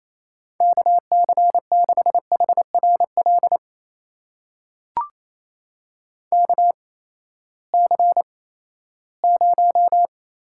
Morse code, heard as a keyed tone, about 28 wpm, spaced as for 16 wpm, 700 Hz. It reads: KC6HRL KC0
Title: USA Call Signs - 6 Characters - 8x Character Spacing - 28wpm